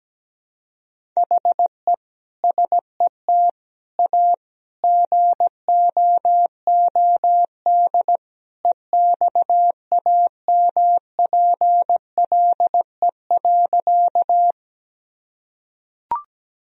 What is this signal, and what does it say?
Morse code, heard as a keyed tone, about 17 wpm, 700 Hz. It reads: HE SET A GOOD EXAMPLE.
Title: Sentences from Top 1000 Words - Review Part 1 - 17wpm